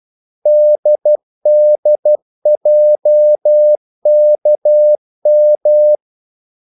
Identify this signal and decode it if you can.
Morse code, heard as a keyed tone, about 12 wpm, 600 Hz. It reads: DDJKM
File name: code-exemple